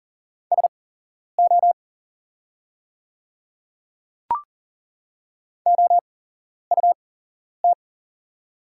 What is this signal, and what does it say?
Morse code, heard as a keyed tone, about 40 wpm, spaced as for 11 wpm, 700 Hz. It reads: SO OUT
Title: 8x Character Spacing - Top 100 Words - 40wpm